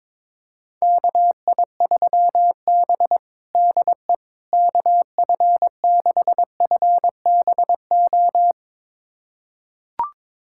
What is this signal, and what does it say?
Morse code, heard as a keyed tone, about 22 wpm, 700 Hz. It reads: KI3B DE KF6FBO
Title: USA Callsign DE Callsign 22wpm